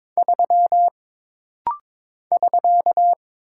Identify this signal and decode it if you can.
Morse code, heard as a keyed tone, about 22 wpm, 700 Hz. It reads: 3 <SK>
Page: This rapid-fire timing is geared to push your ICR (Instant Character Recognition) skills.